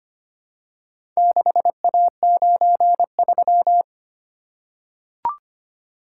Morse code, heard as a keyed tone, about 25 wpm, 700 Hz. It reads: 6A93
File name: practice / 25wpm / Four Letter-Number Combinations - 25wpm